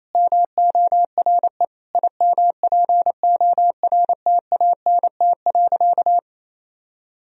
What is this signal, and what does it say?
Morse code, heard as a keyed tone, about 28 wpm, 700 Hz. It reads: MORE IMPORTANT.
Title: Sentences from Top 1000 Words - Review Part 3 - 28wpm